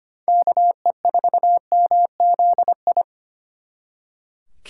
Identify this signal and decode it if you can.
Morse code, heard as a keyed tone, about 33 wpm, 700 Hz. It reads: KE4MZI